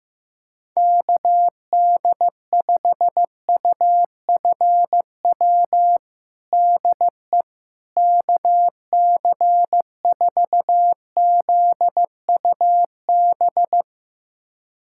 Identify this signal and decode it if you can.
Morse code, heard as a keyed tone, about 16 wpm, 700 Hz. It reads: KD5UFW DE KC4ZUB
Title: USA Callsign DE Callsign - 6 Characters - 15wpm